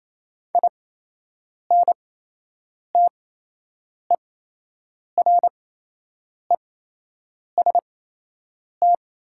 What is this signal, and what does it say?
Morse code, heard as a keyed tone, about 28 wpm, spaced as for 8 wpm, 700 Hz. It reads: INTEREST
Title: 8x Character Spacing - Top 500 Words - 28wpm